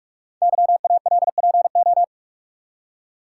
Morse code, heard as a keyed tone, about 44 wpm, 700 Hz. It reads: YALPO
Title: Code Groups - 5 Letters - 45wpm